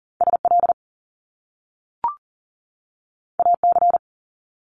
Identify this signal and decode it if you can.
Morse code, heard as a keyed tone, about 39 wpm, 700 Hz. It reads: SL AC
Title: Top Two Letter Combinations in Words 40wpm